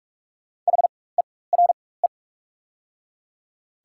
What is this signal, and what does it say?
Morse code, heard as a keyed tone, about 44 wpm, spaced as for 21 wpm, 690 Hz. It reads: HERE
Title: 4x Character Spacing - Top 200 Words - 45wpm